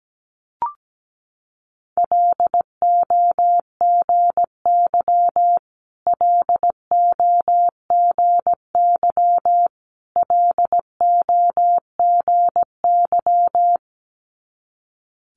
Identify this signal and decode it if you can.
Morse code, heard as a keyed tone, about 17 wpm, 690 Hz. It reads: LOGY LOGY LOGY
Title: Multi-letter Phonemes Repeated 3x 17wpm